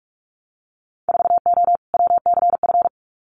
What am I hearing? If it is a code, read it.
Morse code, heard as a keyed tone, about 46 wpm, 690 Hz. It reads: 4O WCF
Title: Winter Field Day Exchanges Encore 45wpm